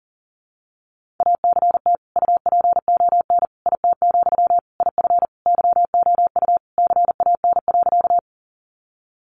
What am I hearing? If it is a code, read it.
Morse code, heard as a keyed tone, about 40 wpm, 700 Hz. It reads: ACT UPON IT, IF YOU CAN.
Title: Sentences from Top 1100 Words 40wpm